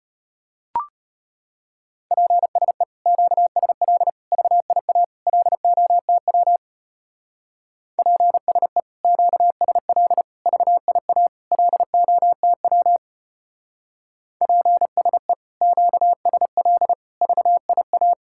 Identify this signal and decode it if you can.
Morse code, heard as a keyed tone, about 32 wpm, 700 Hz. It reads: PSE QSL VIA LOTW PSE QSL VIA LOTW PSE QSL VIA